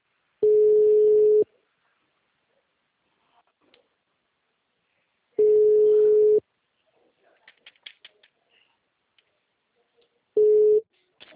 Ringback tone (Europe, 425 Hz).
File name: out- -2017- - - .23434